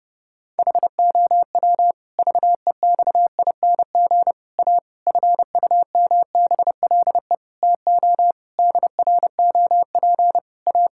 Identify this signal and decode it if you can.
Morse code, heard as a keyed tone, about 30 wpm, 700 Hz. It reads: HOW VEXING A FUMBLE TO DROP A
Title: Pangram Challenge v2 - 30wpm